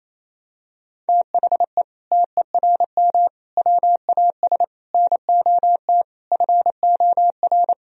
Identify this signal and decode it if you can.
Morse code, heard as a keyed tone, about 27 wpm, 700 Hz. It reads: THE TERM WAS NOT FOR
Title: Sentences from Top 1000 Words - Review Part 4 - 28wpm